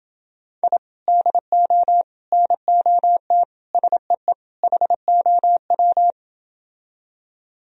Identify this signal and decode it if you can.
Morse code, heard as a keyed tone, about 29 wpm, 700 Hz. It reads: I DO NOT SEE HOW